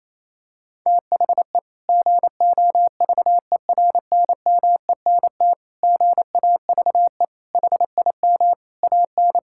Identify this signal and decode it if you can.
Morse code, heard as a keyed tone, about 28 wpm, 700 Hz. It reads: THE GOVERNMENT GAVE HIM AN
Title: Sentences from Top 1000 Words - Review Part 5 - 28wpm